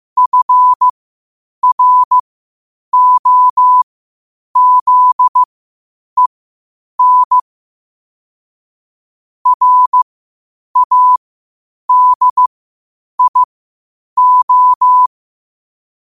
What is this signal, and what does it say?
Morse code, heard as a keyed tone, about 15 wpm, spaced as for 8 wpm, 1000 Hz. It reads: FROZEN RADIO
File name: qotdfiles / 8wpm